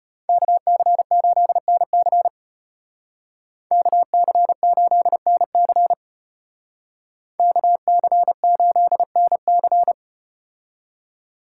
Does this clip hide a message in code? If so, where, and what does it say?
Yes: Morse code from 0.3 to 9.9 s, heard as a keyed tone, about 35 wpm, 700 Hz. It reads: KC8NC KC8NC KC8NC